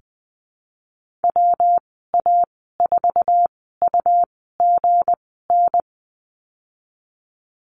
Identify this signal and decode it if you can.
Morse code, heard as a keyed tone, about 20 wpm, spaced as for 15 wpm, 700 Hz. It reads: WA4UGN
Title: 173 Lesson 28g - Number 9 - Call Signs 20wpm